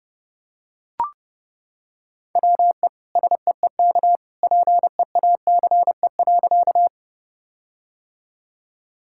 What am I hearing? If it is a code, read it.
Morse code, heard as a keyed tone, about 30 wpm, 700 Hz. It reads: WE SEEK PEACE.
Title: Sentences from Top 1600 Words 30wpm